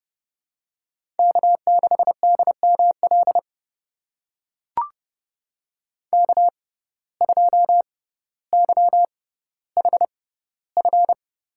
Morse code, heard as a keyed tone, about 30 wpm, spaced as for 15 wpm, 700 Hz. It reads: K6DML K2YHF
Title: USA Call Signs - 5 Characters - 6x Character Spacing - 30wpm